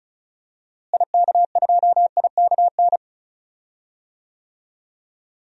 Morse code, heard as a keyed tone, about 36 wpm, 700 Hz. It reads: IK2IKN
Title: DXCC-100 Call Signs - 6 Characters - Repeated 3x - 35wpm